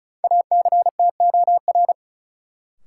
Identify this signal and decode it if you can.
Morse code, heard as a keyed tone, about 36 wpm, 700 Hz. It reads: ACTOR
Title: Sentences from Top 1500 Words - New Word Drill - 35wpm